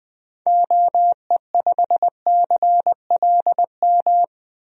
Morse code, heard as a keyed tone, about 20 wpm, 700 Hz. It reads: OE5CLM
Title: DXCC-100 Call Signs - 6 Characters - 20wpm